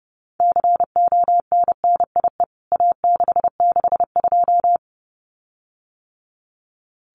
Morse code, heard as a keyed tone, about 30 wpm, 700 Hz. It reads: CONNIE A662
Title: CWT Exchange 30wpm